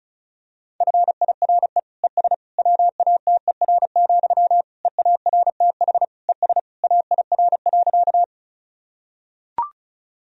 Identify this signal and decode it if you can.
Morse code, heard as a keyed tone, about 36 wpm, 700 Hz. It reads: FIRE ES WATER, EARTH ES AIR.
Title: Sentences from Top 400 Words and Abbreviations 35wpm